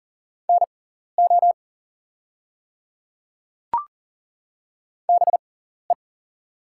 Morse code, heard as a keyed tone, about 40 wpm, spaced as for 14 wpm, 690 Hz. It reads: NO BE
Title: 6x Character Spacing - Top 200 Words - 40wpm